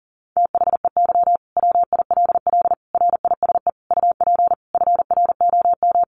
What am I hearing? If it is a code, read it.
Morse code, heard as a keyed tone, about 40 wpm, 700 Hz. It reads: THEY WILL RISE UP FROM
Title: Sentences from Top 900 Words 40wpm